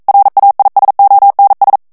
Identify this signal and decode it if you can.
Morse code, heard as a keyed tone, about 42 wpm, 800 Hz. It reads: RAISONS
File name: FRANCAIS TEXTE CLAIR 1 VITESSE 40 MOTS MINUTES